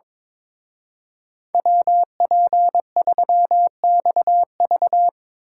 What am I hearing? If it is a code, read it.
Morse code, heard as a keyed tone, about 26 wpm, 700 Hz. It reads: WP3XV
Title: Speed Racing - USA 2x2 Call Signs 22wpm